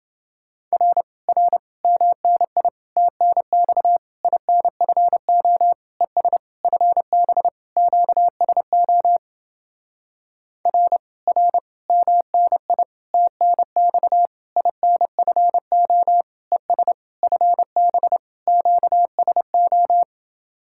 Morse code, heard as a keyed tone, about 29 wpm, 700 Hz. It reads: R R MNI TNX INFO ES FB QSO R R MNI TNX INFO ES FB QSO